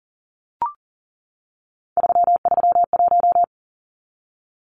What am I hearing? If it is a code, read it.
Morse code, heard as a keyed tone, about 40 wpm, 700 Hz. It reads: 331
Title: Numbers - 3-Digit 40wpm